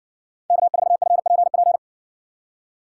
Morse code, heard as a keyed tone, about 60 wpm, 700 Hz. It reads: K4FPP